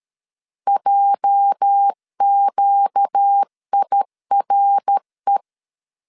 Morse code, heard as a keyed tone, about 13 wpm, 800 Hz. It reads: JQIRE
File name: CW / DET / 1 9 1500C S C 12wpm 3811